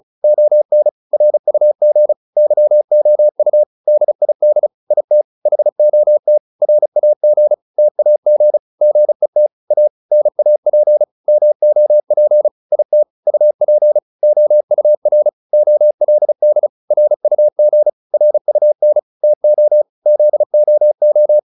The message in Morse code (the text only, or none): ON RUG YOU DID IT HOT RAG TAG GET A NAP MOP IT UP OUR OLD RUG RUN TO ZOO